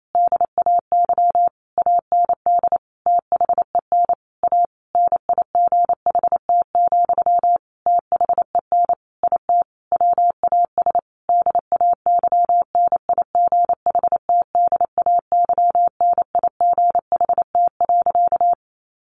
Morse code, heard as a keyed tone, about 28 wpm, 700 Hz. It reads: DAY AND THEN A NIGHT, THEN IT WAS DAYNIGHTDAYNIGHT.
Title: A Sound Of Thunder 28wpm